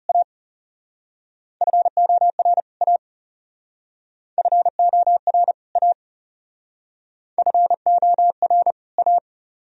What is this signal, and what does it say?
Morse code, heard as a keyed tone, about 37 wpm, 700 Hz. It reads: A FOR A FOR A FOR A
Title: Speed-Racing - Sets of 2 Words 30wpm